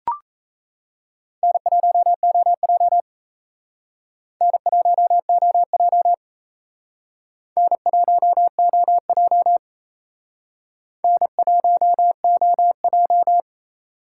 Morse code, heard as a keyed tone, about 35 wpm, 700 Hz. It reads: N1OJ N1OJ N1OJ N1OJ
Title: Speed Racing - USA 1x2 Call Signs 28wpm